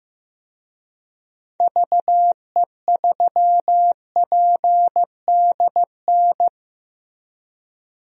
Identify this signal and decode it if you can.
Morse code, heard as a keyed tone, about 15 wpm, 700 Hz. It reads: VE3PDN
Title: DXCC-100 Call Signs - 6 Characters - Repeated 3x - 15wpm